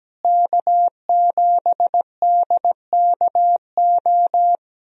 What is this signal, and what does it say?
Morse code, heard as a keyed tone, about 17 wpm, 700 Hz. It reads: K7DKO
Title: USA Call Signs - 5 Characters - Repeated 2x - 17wpm